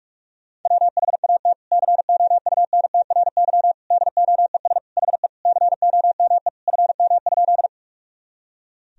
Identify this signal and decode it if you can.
Morse code, heard as a keyed tone, about 41 wpm, 700 Hz. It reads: WHAT COUNTRY DOES HE COME FM?
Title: Sentences from Top 300 Words and Abbreviations 45wpm